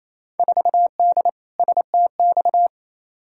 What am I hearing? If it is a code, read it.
Morse code, heard as a keyed tone, about 28 wpm, 700 Hz. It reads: 4D STX
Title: Field Day Exchanges Encore - 28wpm